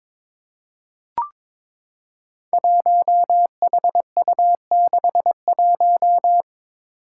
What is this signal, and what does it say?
Morse code, heard as a keyed tone, about 22 wpm, 700 Hz. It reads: 1HU61
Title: Five Challenging Letter-Number Combinations 22wpm